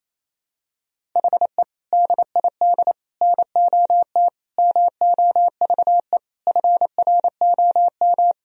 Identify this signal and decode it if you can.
Morse code, heard as a keyed tone, about 28 wpm, 700 Hz. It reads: HE DID NOT MOVE FROM
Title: Sentences from Top 1000 Words - Review Part 1 - 28wpm